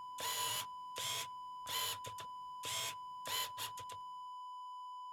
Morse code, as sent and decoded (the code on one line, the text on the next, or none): none